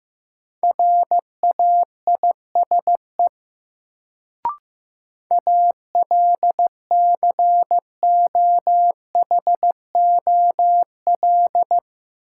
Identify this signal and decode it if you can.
Morse code, heard as a keyed tone, about 15 wpm, 700 Hz. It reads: RAISE ALCOHOL
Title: Sentences from Top 1400 Words - New Word Drill - 15wpm